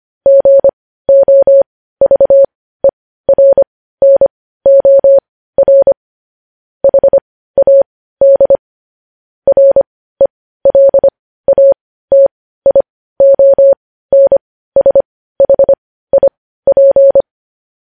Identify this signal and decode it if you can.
Morse code, heard as a keyed tone, about 25 wpm, spaced as for 15 wpm, 550 Hz. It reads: GOVERNOR HAD RELATIONSHIP
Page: News Headlines in Morse Code at 15 WPM